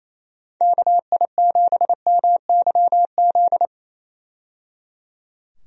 Morse code, heard as a keyed tone, about 28 wpm, 700 Hz. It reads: KI7MYZ
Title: USA Call Signs - 6 Characters - 28wpm